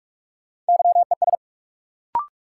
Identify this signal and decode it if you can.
Morse code, heard as a keyed tone, about 46 wpm, 700 Hz. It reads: YES
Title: Sentences from Top 500 Words - New Word Drill - 45wpm